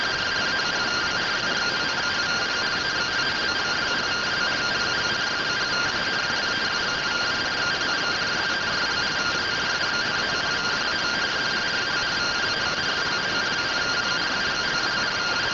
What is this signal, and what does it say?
Two-tone teleprinter signal, about 170 Hz shift.
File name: findings / MR AXIAL 71mm 3mm.gen-41 rtty rx